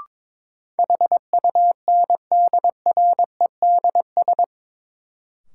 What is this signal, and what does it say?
Morse code, heard as a keyed tone, about 22 wpm, 700 Hz. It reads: HUNDREDS
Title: Sets of 4 Words - New Word Drill - 22wpm